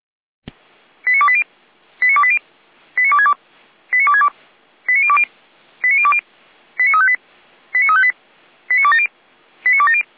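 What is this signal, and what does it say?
Five-tone selective call (ZVEI), heard as quick runs of five tones, tones of about 70 ms. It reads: 89280 89280 89262 89262 89020 89020 89358 89358 89270 89270
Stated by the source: ZVEI five-tone selective call 89230 received on an SDR, from the Feuerwehr Orlamünde audio archive.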